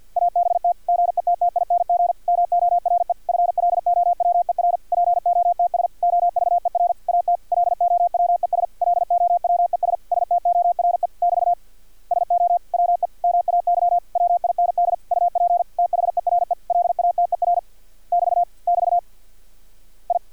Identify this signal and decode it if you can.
Morse code, heard as a keyed tone, about 50 wpm, 700 Hz. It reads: ABT GETTING MORE FLOWER POTS OVER AT LOWES LOWES STORE <BT> SO WE MAY WIND UP THERE LATER <BT> <BT> I
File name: archived / CRSnet